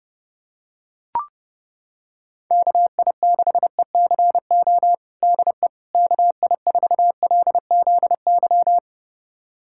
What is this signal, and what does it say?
Morse code, heard as a keyed tone, about 30 wpm, 700 Hz. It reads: KI6ECO DE KI4LZY
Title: USA Callsign DE Callsign - 6 Characters - 30wpm